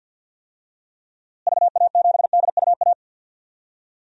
Morse code, heard as a keyed tone, about 47 wpm, 700 Hz. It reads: VA7DUA
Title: DXCC-100 Call Signs - 6 Characters - Repeated 2x - 50wpm